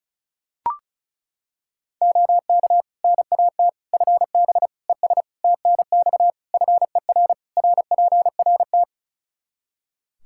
Morse code, heard as a keyed tone, about 34 wpm, 700 Hz. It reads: OK NAT FB ES TNX FER RPRT
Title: Practice QSO - Exchange 2 - Info - 35wpm